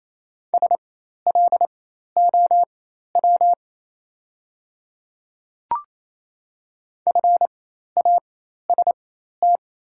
Morse code, heard as a keyed tone, about 27 wpm, spaced as for 13 wpm, 690 Hz. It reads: SLOW FAST
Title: 4x Character Spacing - Top 500 Words - 28wpm